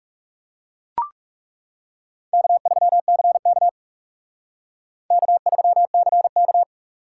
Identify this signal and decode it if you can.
Morse code, heard as a keyed tone, about 34 wpm, 700 Hz. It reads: K3CK K3CK
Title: Speed Racing - USA 1x2 Call Signs 30wpm